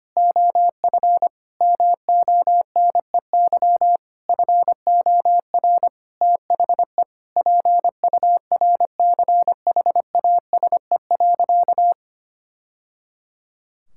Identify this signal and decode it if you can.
Morse code, heard as a keyed tone, about 25 wpm, 700 Hz. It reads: OF MONEY FOR THE PURCHASE.
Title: Sentences from Top 1000 Words - Review Part 5 - 25wpm